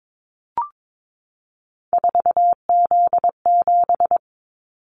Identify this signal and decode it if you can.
Morse code, heard as a keyed tone, about 22 wpm, 700 Hz. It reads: 4Z7
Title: Callsign Prefixes for Top 50 DXCC - 22wpm